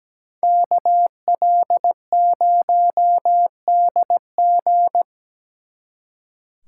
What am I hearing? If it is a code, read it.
Morse code, heard as a keyed tone, about 20 wpm, 700 Hz. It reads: KL0DG
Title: Speed Racing - USA 2x2 Call Signs 17wpm